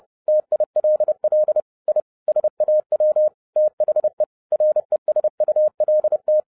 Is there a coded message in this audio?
Yes: Morse code from 0.3 to 6.4 s, heard as a keyed tone, about 30 wpm, 610 Hz. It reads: TILL I SAW THE RESULT